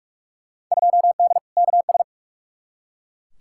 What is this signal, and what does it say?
Morse code, heard as a keyed tone, about 44 wpm, 700 Hz. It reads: 2D KS
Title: Field Day Exchanges Encore - 45wpm